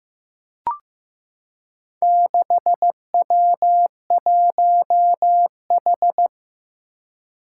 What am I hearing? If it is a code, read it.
Morse code, heard as a keyed tone, about 15 wpm, 700 Hz. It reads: 6W1H
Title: Four Challenging Letter-Number Combinations 15wpm